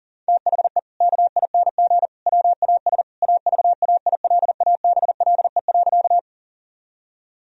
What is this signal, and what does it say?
Morse code, heard as a keyed tone, about 40 wpm, 700 Hz. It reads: THE KING WAS AVAILABLE.
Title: Sentences from Top 1000 Words - Review Part 3 - 40wpm